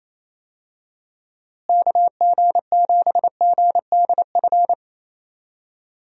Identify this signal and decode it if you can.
Morse code, heard as a keyed tone, about 27 wpm, 700 Hz. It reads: KG7GDF
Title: USA Call Signs - 6 Characters - Repeated 3x - 28wpm